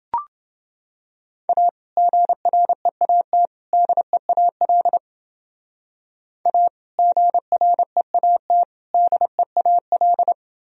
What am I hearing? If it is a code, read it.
Morse code, heard as a keyed tone, about 29 wpm, 700 Hz. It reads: A GREAT DEAL A GREAT DEAL